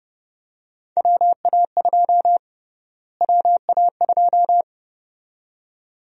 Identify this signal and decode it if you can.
Morse code, heard as a keyed tone, about 32 wpm, 700 Hz. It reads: WA2 WA2
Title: USA Call Sign Prefixes - Repeated 2x - 30wpm